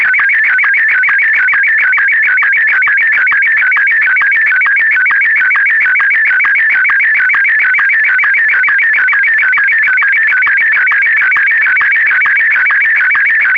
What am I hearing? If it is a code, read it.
SSTV (slow-scan television), Martin 1.
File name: sstv / sstv2 / martin m1